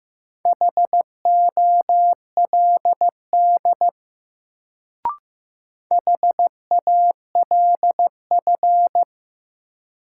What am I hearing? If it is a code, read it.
Morse code, heard as a keyed tone, about 15 wpm, 700 Hz. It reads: HOLD HALF
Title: Sentences from Top 400 Words - New Word Drill - 15wpm